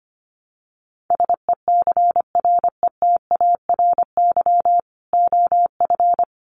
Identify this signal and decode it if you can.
Morse code, heard as a keyed tone, about 25 wpm, 700 Hz. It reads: SECRETARY OF
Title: Sentences from Top 1300 Words 25wpm